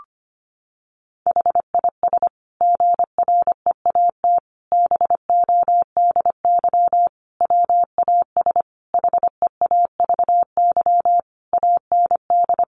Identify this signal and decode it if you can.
Morse code, heard as a keyed tone, about 25 wpm, 700 Hz. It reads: HIS GREAT BODY WAS HEAVY AND
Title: Sentences from Top 1400 Words 25wpm